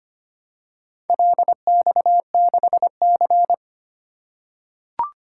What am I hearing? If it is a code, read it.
Morse code, heard as a keyed tone, about 25 wpm, 700 Hz. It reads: LX6C
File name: practice / 25wpm / Four Letter-Number Combinations - 25wpm